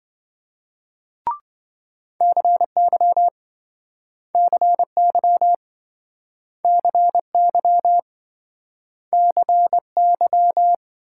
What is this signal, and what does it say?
Morse code, heard as a keyed tone, about 28 wpm, 700 Hz. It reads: CY CY CY CY